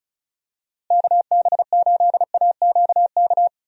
Morse code, heard as a keyed tone, about 36 wpm, 700 Hz. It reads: KB8AQK
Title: USA Call Signs - 6 Characters - Repeated 3x - 35wpm